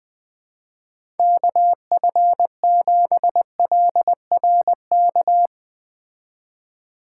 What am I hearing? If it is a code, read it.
Morse code, heard as a keyed tone, about 20 wpm, 700 Hz. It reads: KF7LRK
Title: Speed Racing - USA 2x3 Call Signs 17wpm